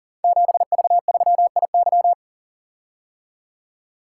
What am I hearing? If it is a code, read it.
Morse code, heard as a keyed tone, about 39 wpm, 700 Hz. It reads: 7V3IY
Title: Code Groups - 5 Letter:Numbers - 40wpm